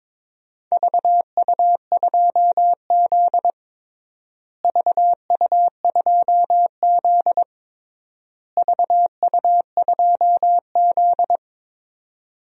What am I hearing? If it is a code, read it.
Morse code, heard as a keyed tone, about 22 wpm, 700 Hz. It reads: VU2Z VU2Z VU2Z
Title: DXCC-100 Call Signs - 4 Characters - Repeated 3x - 22wpm